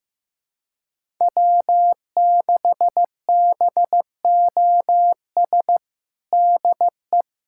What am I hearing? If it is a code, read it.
Morse code, heard as a keyed tone, about 15 wpm, 700 Hz. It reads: W6BOS DE
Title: USA Callsign DE Callsign - 5 Characters - 15wpm